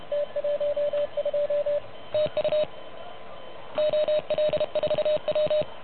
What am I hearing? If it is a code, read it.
Morse code, heard as a keyed tone, about 30 wpm, 590 Hz. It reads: T12 TU OL4W
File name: truSDX / MWC29.8